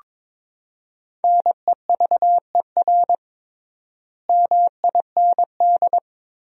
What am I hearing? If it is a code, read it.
Morse code, heard as a keyed tone, about 21 wpm, 700 Hz. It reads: NEVER MIND
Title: 3x Word-Spacing - Sentences from Top 500 Words - 22wpm